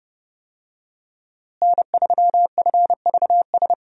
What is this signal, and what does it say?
Morse code, heard as a keyed tone, about 30 wpm, 700 Hz. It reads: N3FVS
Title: USA Call Signs - 5 Characters - Repeated 2x - 30wpm